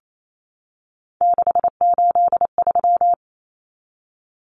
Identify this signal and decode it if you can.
Morse code, heard as a keyed tone, about 28 wpm, 700 Hz. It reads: 683
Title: Numbers - 3-Digit 28wpm